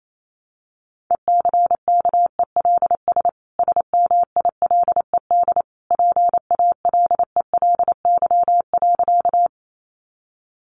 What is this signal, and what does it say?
Morse code, heard as a keyed tone, about 28 wpm, 700 Hz. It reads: ECKELS SMILED PALELY.
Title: A Sound Of Thunder 28wpm